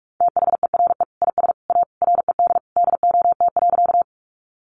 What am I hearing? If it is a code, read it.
Morse code, heard as a keyed tone, about 44 wpm, 700 Hz. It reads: THERE IS A RED DOT.
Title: Sentences from Top 1500 Words 45wpm